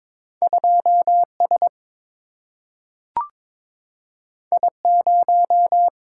Morse code, heard as a keyed tone, about 21 wpm, 700 Hz. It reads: 2 S I 0
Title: Single-Letter Single-Number 22wpm